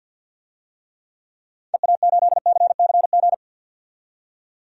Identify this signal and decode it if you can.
Morse code, heard as a keyed tone, about 46 wpm, 700 Hz. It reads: EA8CCG
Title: DXCC-100 Call Signs - 6 Characters - 50wpm